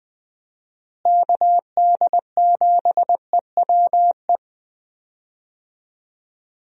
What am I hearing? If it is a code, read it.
Morse code, heard as a keyed tone, about 20 wpm, 700 Hz. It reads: KD7EWE
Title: USA Call Signs - 6 Characters - Repeated 3x - 20wpm